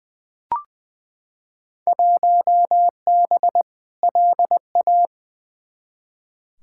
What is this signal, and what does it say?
Morse code, heard as a keyed tone, about 20 wpm, 700 Hz. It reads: 1B LA
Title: Field Day Exchanges - 20wpm